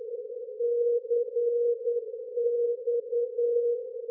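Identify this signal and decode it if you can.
Morse code, heard as a keyed tone, about 10 wpm, 480 Hz. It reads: CX
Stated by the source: Non-directional aviation beacon on 219 kHz in the Harrisburg PA area, using a 3-wire T antenna.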